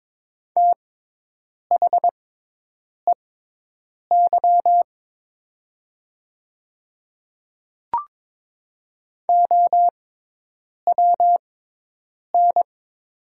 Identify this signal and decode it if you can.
Morse code, heard as a keyed tone, about 22 wpm, spaced as for 8 wpm, 700 Hz. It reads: THEY OWN
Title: 6x Character Spacing - Top 500 Words - 22wpm